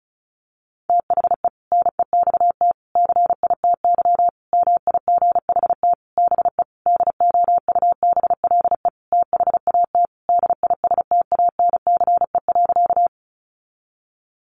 Morse code, heard as a keyed tone, about 34 wpm, 700 Hz. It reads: THE NEXT CITY MIGHT BE DOUBLE THAT DISTANCE.
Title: Sentences from Top 1400 Words 35wpm